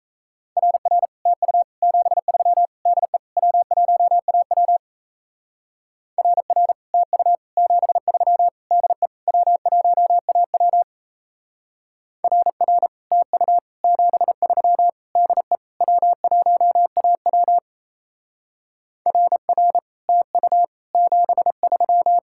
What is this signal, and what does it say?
Morse code, heard as a keyed tone, about 34 wpm, 700 Hz. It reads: RR TU 73 DE W1AW RR TU 73 DE W1AW RR TU 73 DE W1AW RR TU 73
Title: Speed-Racing Common QSO Phrases Encore - 28wpm